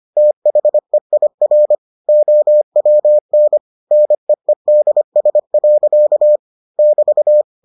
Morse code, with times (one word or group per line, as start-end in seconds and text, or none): 0.2-1.7 THEIR
2.1-3.6 OWN
3.9-6.4 NEEDS.
6.8-7.4 <BT>